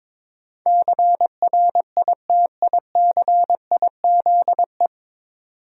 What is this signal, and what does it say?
Morse code, heard as a keyed tone, about 22 wpm, 700 Hz. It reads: CRITICIZE
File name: 9 Letter Words - First 3 Letters Given - 22wpm